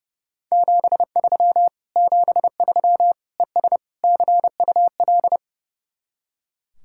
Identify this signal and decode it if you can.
Morse code, heard as a keyed tone, about 37 wpm, 700 Hz. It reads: 73 73 ES CUL
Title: Speed-Racing Common QSO Phrases - 30wpm